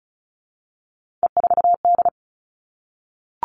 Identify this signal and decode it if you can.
Morse code, heard as a keyed tone, about 34 wpm, 700 Hz. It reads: E4D
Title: Three Challenging Letter-Number Combinations 35wpm